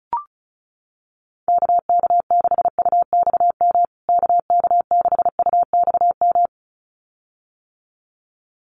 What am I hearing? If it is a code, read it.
Morse code, heard as a keyed tone, about 34 wpm, 700 Hz. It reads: KK6UXM KK6UXM
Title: USA Call Signs Repeated 2x 35wpm